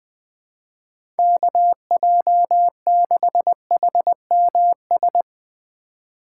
Morse code, heard as a keyed tone, about 21 wpm, 700 Hz. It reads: KJ6HMS